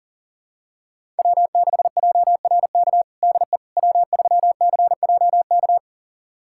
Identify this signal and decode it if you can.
Morse code, heard as a keyed tone, about 40 wpm, 700 Hz. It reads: W6JRK DE W3CJK
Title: USA Callsign DE Callsign - 5 Characters - 40wpm